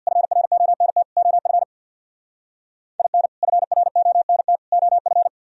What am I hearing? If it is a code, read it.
Morse code, heard as a keyed tone, about 60 wpm, 700 Hz. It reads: FRONT OF IN FRONT OF
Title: Speed-Racing - Sets of 3 Words 50wpm